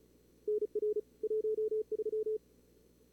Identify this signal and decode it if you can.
Morse code, heard as a keyed tone, about 36 wpm, 430 Hz. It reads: NR 13
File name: CWSprint / Feb2022 / k5tr-feb-na-sprint-cw-k5ot -2300